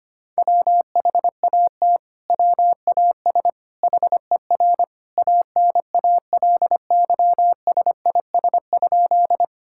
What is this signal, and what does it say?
Morse code, heard as a keyed tone, about 25 wpm, 700 Hz. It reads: WHAT WAS HER ANALYSIS?
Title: Sentences from Top 1000 Words - Review Part 5 - 25wpm